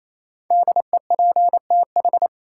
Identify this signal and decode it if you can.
Morse code, heard as a keyed tone, about 28 wpm, 700 Hz. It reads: DEPTH
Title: Sentences from Top 1200 Words - New Word Drill Alt - 28wpm